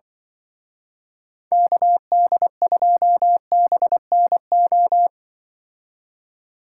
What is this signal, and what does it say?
Morse code, heard as a keyed tone, about 24 wpm, 700 Hz. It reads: KD2BNO
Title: Speed Racing - USA 2x3 Call Signs 20wpm